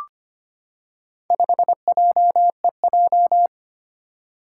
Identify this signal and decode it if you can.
Morse code, heard as a keyed tone, about 25 wpm, 700 Hz. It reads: 5JEJ
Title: Four Challenging Letter-Number Combinations 25wpm